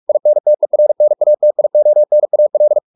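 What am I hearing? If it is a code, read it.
Morse code, heard as a keyed tone, about 44 wpm, 600 Hz. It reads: INTERNATIONAL
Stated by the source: Morse practice at 45 wpm, the novel Never by Ken Follett in English, from CW med Gnister.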